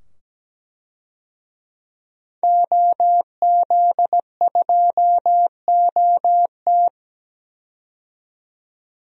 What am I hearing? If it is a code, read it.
Morse code, heard as a keyed tone, about 17 wpm, 700 Hz. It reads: OZ2OT
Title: DXCC-100 Call Signs - 5 Characters - 17wpm